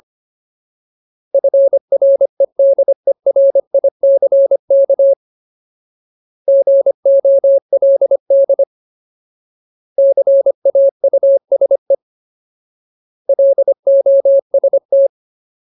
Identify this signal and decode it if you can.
Morse code, heard as a keyed tone, about 25 wpm, 550 Hz. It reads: FREDERICK GOLD CAUSE LOST
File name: Ultimate Word List / 25WPM Ultimate